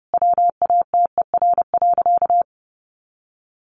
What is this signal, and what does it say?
Morse code, heard as a keyed tone, about 30 wpm, 700 Hz. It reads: WATER.
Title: A Sound Of Thunder 30wpm